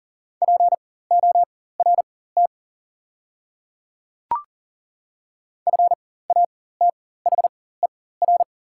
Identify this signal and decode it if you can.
Morse code, heard as a keyed tone, about 40 wpm, spaced as for 19 wpm, 690 Hz. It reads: PORT FATHER
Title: 4x Character Spacing - Top 200 Words - 40wpm